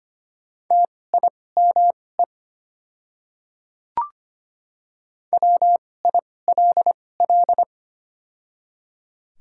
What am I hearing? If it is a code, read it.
Morse code, heard as a keyed tone, about 25 wpm, spaced as for 18 wpm, 690 Hz. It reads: TIME WILL
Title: 2x Character Spacing - Top 100 Words - 25wpm